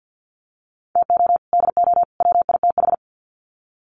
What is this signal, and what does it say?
Morse code, heard as a keyed tone, about 50 wpm, 700 Hz. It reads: TO DO WITH
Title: Sets of 3 Words 50wpm